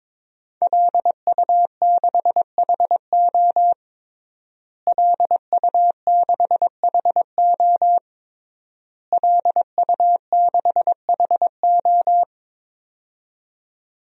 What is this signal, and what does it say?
Morse code, heard as a keyed tone, about 22 wpm, 700 Hz. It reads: LU6HO LU6HO LU6HO
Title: DXCC-100 Call Signs - 5 Characters - Repeated 3x - 22wpm